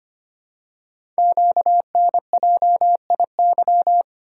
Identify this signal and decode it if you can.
Morse code, heard as a keyed tone, about 25 wpm, 700 Hz. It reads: QNJIY
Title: Code Groups - 5 Letters - 25wpm